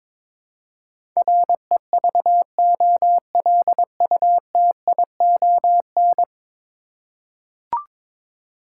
Morse code, heard as a keyed tone, about 22 wpm, 700 Hz. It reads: REVOLUTION
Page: morse-code-ninja-practice-22wpm